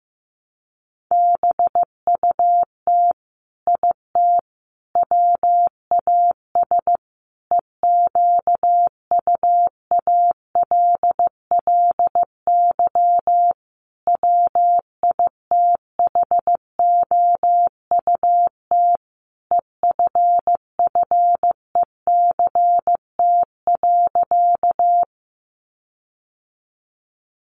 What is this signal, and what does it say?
Morse code, heard as a keyed tone, about 15 wpm, 700 Hz. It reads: BUT IT WAS EQUALLY WITHOUT EFFECT.
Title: Sentences from Top 1400 Words 15wpm